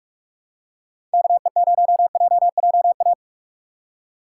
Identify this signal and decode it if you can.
Morse code, heard as a keyed tone, about 46 wpm, 700 Hz. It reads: KE0JJA
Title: USA Call Signs - 3x Character Spacing - 45wpm